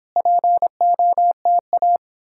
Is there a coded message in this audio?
Yes: Morse code from 0.2 to 2.0 s, heard as a keyed tone, about 29 wpm, 700 Hz. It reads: POTA